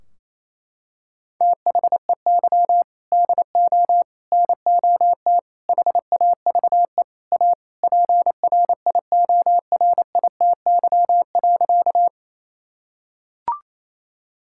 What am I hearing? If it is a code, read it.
Morse code, heard as a keyed tone, about 27 wpm, 700 Hz. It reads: THEY DO NOT HAVE A PRIORITY.
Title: Sentences from Top 1600 Words 28wpm